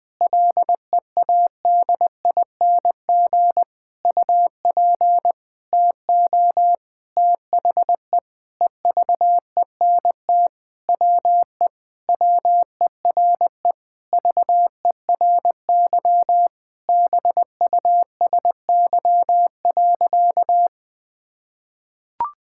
Morse code, heard as a keyed tone, about 21 wpm, 700 Hz. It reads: LEADING UP TO THE EVENT WE WERE VERY BUSY.
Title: Sentences with 7-Word Phrases - 20wpm